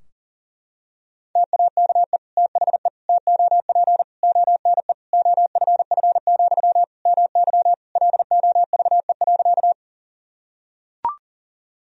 Morse code, heard as a keyed tone, about 40 wpm, 700 Hz. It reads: TAKE THE TOP ONE OFF, MY LOVE.
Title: Sentences from Top 1000 Words - Review Part 2 - 40wpm